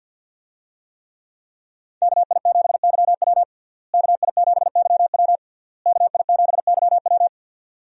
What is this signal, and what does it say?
Morse code, heard as a keyed tone, about 52 wpm, 700 Hz. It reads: KI7YW KI7YW KI7YW
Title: USA Call Signs - 5 Characters - Repeated 3x - 50wpm